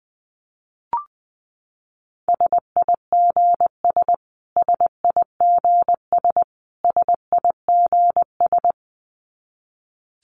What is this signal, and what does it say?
Morse code, heard as a keyed tone, about 20 wpm, 700 Hz. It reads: SIGS SIGS SIGS
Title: Abbreviations Repeated 3x - 20wpm